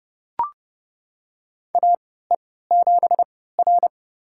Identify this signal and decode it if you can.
Morse code, heard as a keyed tone, about 30 wpm, spaced as for 17 wpm, 700 Hz. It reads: AE7R
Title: USA Call Signs - 4 Characters - 3x Character Spacing - 30wpm